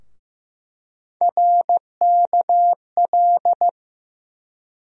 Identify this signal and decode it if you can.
Morse code, heard as a keyed tone, about 15 wpm, 700 Hz. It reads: RKL
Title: USA Call Sign Postfixes - 3 Letters 15wpm